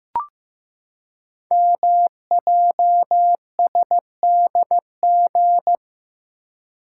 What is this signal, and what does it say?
Morse code, heard as a keyed tone, about 15 wpm, 700 Hz. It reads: MJSDG
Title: Code Groups - 5 Letters - 15wpm